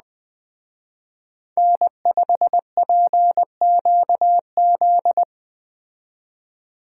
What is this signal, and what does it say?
Morse code, heard as a keyed tone, about 20 wpm, 700 Hz. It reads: N5PQZ
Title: Speed Racing - USA 1x3 Call Signs 17wpm